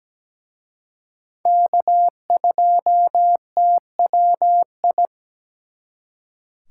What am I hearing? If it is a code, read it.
Morse code, heard as a keyed tone, about 20 wpm, 700 Hz. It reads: K2TWI
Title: Speed Racing - USA 1x3 Call Signs 17wpm